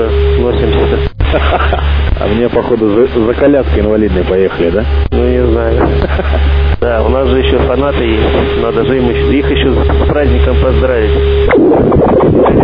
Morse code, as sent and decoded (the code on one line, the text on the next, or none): none